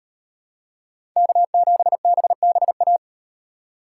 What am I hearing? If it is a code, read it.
Morse code, heard as a keyed tone, about 34 wpm, 700 Hz. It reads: K7BBA